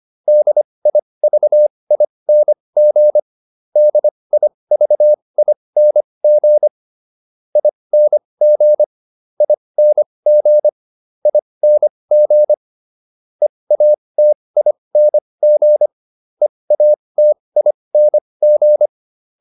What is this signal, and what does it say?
Morse code, heard as a keyed tone, about 25 wpm, spaced as for 20 wpm, 600 Hz. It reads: DIVING DIVING ING ING ING EATING EATING